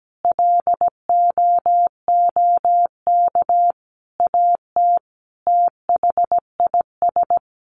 Morse code, heard as a keyed tone, about 17 wpm, 700 Hz. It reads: LOOK AT THIS
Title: Sets of 5 Words 17wpm